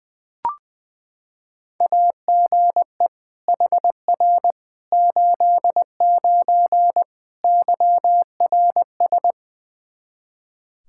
Morse code, heard as a keyed tone, about 20 wpm, 700 Hz. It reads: AGE HR 89 YRS
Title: Practice QSO - Exchange 3 - More Info - 20wpm